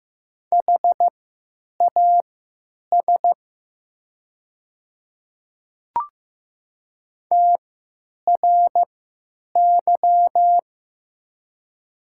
Morse code, heard as a keyed tone, about 15 wpm, spaced as for 9 wpm, 700 Hz. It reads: HAS TRY
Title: 3x Character Spacing - Top 300 Words - 15wpm